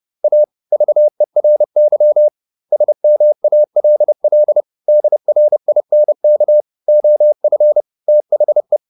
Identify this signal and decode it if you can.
Morse code, heard as a keyed tone, about 30 wpm, 600 Hz. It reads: A VERY SMALL DRINK OF THE